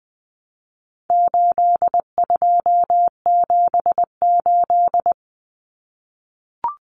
Morse code, heard as a keyed tone, about 20 wpm, 700 Hz. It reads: 8278
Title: Numbers - 4-Digit 20wpm